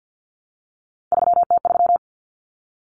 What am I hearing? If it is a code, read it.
Morse code, heard as a keyed tone, about 50 wpm, 700 Hz. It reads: 3T3
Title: Common Cut Numbers - 3-Digit 50wpm